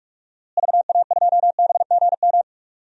Morse code, heard as a keyed tone, about 44 wpm, 700 Hz. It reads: VA2BGM
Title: DXCC-100 Call Signs - 6 Characters - Repeated 2x - 45wpm